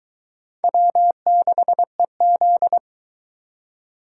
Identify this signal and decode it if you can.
Morse code, heard as a keyed tone, about 20 wpm, 700 Hz. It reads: W6EZ